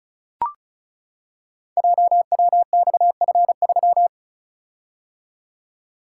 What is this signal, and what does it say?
Morse code, heard as a keyed tone, about 36 wpm, 690 Hz. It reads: JWXF3
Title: Code Groups - 5 Letter:Numbers - 35wpm